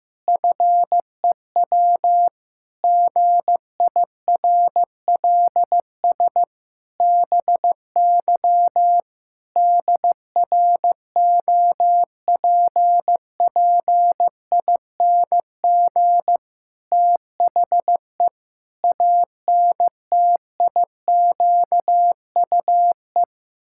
Morse code, heard as a keyed tone, about 15 wpm, 700 Hz. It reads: FEW GIRLS BY DROPPING THE ANTIQUE
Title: Pangram Challenge v2 - 15wpm